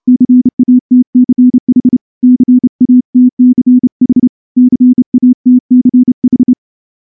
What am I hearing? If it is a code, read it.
Morse code, heard as a keyed tone, about 28 wpm, 250 Hz. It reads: CATCH CATCH CATCH
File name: morse-training-30WPM